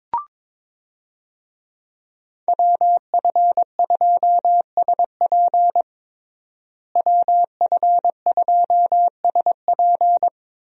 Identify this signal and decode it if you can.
Morse code, heard as a keyed tone, about 22 wpm, 700 Hz. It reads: WF2SP WF2SP
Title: USA Call Signs - 5 Characters - Repeated 3x - 22wpm